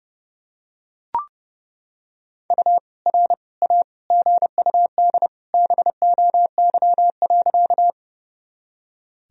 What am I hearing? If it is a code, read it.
Morse code, heard as a keyed tone, about 32 wpm, 700 Hz. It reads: U R A GUD BOY.
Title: Sentences from Top 200 Words and Abbreviations 30wpm